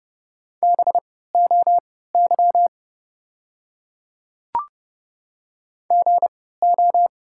Morse code, heard as a keyed tone, about 31 wpm, spaced as for 17 wpm, 690 Hz. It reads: BOY GO
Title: 3x Character Spacing - Top 200 Words - 30wpm